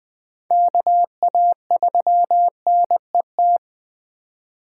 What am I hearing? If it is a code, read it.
Morse code, heard as a keyed tone, about 20 wpm, 700 Hz. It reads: KA3NET
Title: USA Call Signs - 6 Characters - 3x Character Spacing - 20wpm